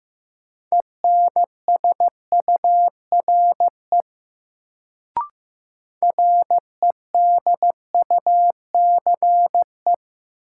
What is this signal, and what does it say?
Morse code, heard as a keyed tone, about 15 wpm, 700 Hz. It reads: ENSURE REDUCE
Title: Sentences from Top 1100 Words - New Word Drill Alt - 15wpm